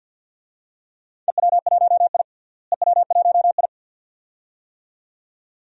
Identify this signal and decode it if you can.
Morse code, heard as a keyed tone, about 48 wpm, 700 Hz. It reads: EW1I EW1I
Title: DXCC-100 Call Signs - 4 Characters - Repeated 2x - 50wpm